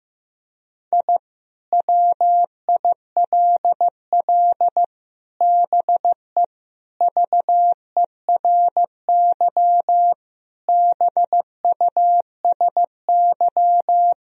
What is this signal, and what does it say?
Morse code, heard as a keyed tone, about 15 wpm, 700 Hz. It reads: I WILL BE VERY BUSY
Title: Sentences from Top 1000 Words - Review Part 3 - 15wpm